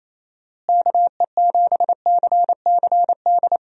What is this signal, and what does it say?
Morse code, heard as a keyed tone, about 28 wpm, 700 Hz. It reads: KE7CCD
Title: USA Call Signs - 6 Characters - Repeated 2x - 28wpm